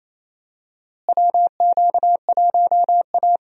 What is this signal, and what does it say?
Morse code, heard as a keyed tone, about 28 wpm, 700 Hz. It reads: WQ1A
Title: USA Call Signs - 4 Characters - Repeated 2x - 28wpm